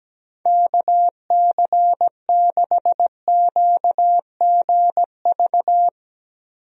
Morse code, heard as a keyed tone, about 17 wpm, 700 Hz. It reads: KC6QGV
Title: USA Call Signs - 6 Characters - Repeated 3x - 17wpm